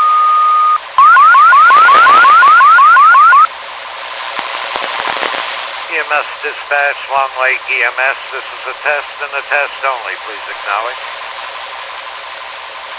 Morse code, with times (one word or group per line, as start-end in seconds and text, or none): none